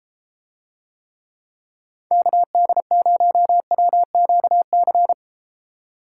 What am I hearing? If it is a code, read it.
Morse code, heard as a keyed tone, about 30 wpm, 700 Hz. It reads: KD0WQC